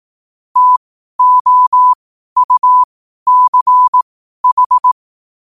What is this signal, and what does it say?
Morse code, heard as a keyed tone, about 18 wpm, spaced as for 12 wpm, 1000 Hz. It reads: TOUCH
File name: qotdfiles / 12wpm